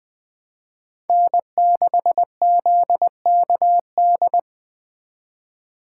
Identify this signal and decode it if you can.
Morse code, heard as a keyed tone, about 23 wpm, 700 Hz. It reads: N6ZKD
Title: Speed Racing - USA 1x3 Call Signs 20wpm